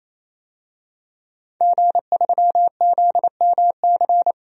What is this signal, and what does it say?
Morse code, heard as a keyed tone, about 28 wpm, 700 Hz. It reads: G3ZMC
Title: DXCC-100 Call Signs - 5 Characters - 28wpm